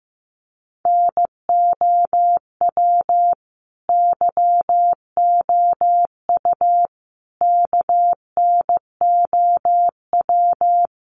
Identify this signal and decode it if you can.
Morse code, heard as a keyed tone, about 15 wpm, 700 Hz. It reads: NOW YOU KNOW
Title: Sentences from Top 1200 Words 15wpm